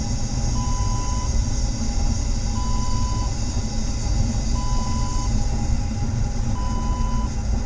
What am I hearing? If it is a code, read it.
There is no code.